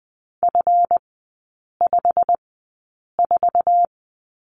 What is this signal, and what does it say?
Morse code, heard as a keyed tone, about 20 wpm, spaced as for 8 wpm, 690 Hz. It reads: F54
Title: 159 Lesson 26e - Letter g - Three Characters 20wpm